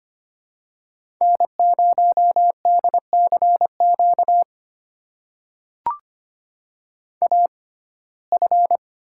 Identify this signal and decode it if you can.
Morse code, heard as a keyed tone, about 25 wpm, spaced as for 18 wpm, 700 Hz. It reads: N0DCQ AF
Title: USA Call Signs - 6x Character Spacing - 25wpm